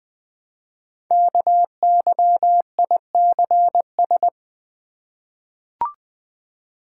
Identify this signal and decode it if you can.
Morse code, heard as a keyed tone, about 20 wpm, 700 Hz. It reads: KYICS
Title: Code Groups - 5 Letters - 20wpm